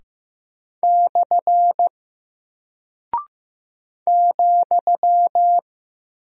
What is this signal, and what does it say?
Morse code, heard as a keyed tone, about 15 wpm, 700 Hz. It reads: / ,
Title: Single Character (Letter:Number:Punctuation:Prosign) Alt 15wpm